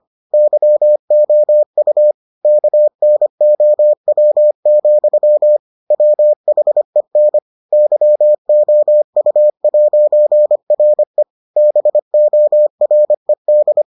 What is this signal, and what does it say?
Morse code, heard as a keyed tone, about 25 wpm, 600 Hz. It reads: YOU KNOW, WHEN YOU'RE BORED